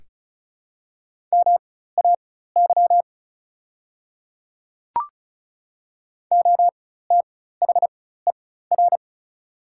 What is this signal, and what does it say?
Morse code, heard as a keyed tone, about 36 wpm, spaced as for 16 wpm, 700 Hz. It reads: MAY OTHER
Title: 4x Character Spacing - Top 200 Words - 35wpm